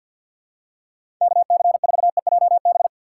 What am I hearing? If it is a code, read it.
Morse code, heard as a keyed tone, about 51 wpm, 700 Hz. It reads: KC4EJB
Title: USA Call Signs - 6 Characters - 4x Character Spacing - 50wpm